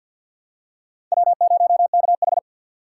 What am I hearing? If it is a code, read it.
Morse code, heard as a keyed tone, about 51 wpm, 700 Hz. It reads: W0KH
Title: USA Call Signs - 6x Character Spacing - 50wpm